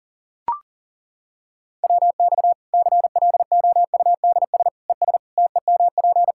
Morse code, heard as a keyed tone, about 40 wpm, 700 Hz. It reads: WX CLOUDS ES TEMP
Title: Practice QSO - Exchange 2 - Info - 40wpm